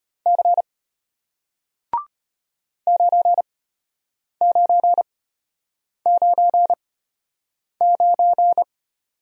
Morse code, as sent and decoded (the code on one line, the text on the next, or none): -.-. / ----. ----. ----. ----.
C 9999